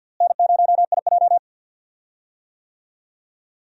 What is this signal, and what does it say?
Morse code, heard as a keyed tone, about 51 wpm, 700 Hz. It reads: N0IJ
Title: USA Call Signs - 4 Characters - Repeated 3x - 50wpm